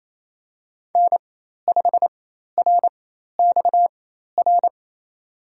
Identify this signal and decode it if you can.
Morse code, heard as a keyed tone, about 28 wpm, spaced as for 13 wpm, 700 Hz. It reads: N5RXR
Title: USA Call Signs - 4x Character Spacing - 28wpm